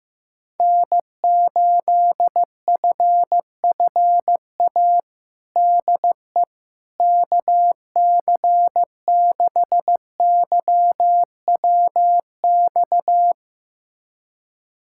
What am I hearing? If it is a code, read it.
Morse code, heard as a keyed tone, about 16 wpm, 700 Hz. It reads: N8FFA DE KC6YWX
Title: USA Callsign DE Callsign 15wpm